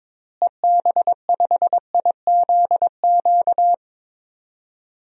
Morse code, heard as a keyed tone, about 22 wpm, 700 Hz. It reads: EB5IZQ
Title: DXCC-100 Call Signs - 6 Characters - Repeated 3x - 22wpm